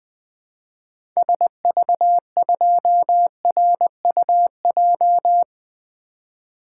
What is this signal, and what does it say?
Morse code, heard as a keyed tone, about 20 wpm, 700 Hz. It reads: SV2RUJ
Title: 2023 SOTA Activator Call Signs - 20wpm